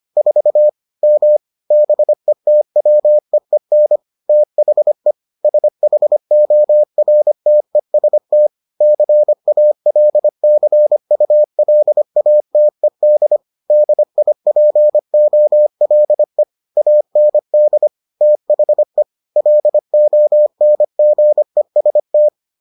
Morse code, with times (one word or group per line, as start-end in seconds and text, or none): 0.2-0.7 4
1.0-1.4 M
1.7-4.0 BETWEEN
4.3-5.1 THE
5.4-8.5 SHORTEST
8.8-13.4 CALCULATED
13.7-16.4 DIPOLE
16.8-17.9 AND
18.2-19.0 THE
19.4-22.3 LONGEST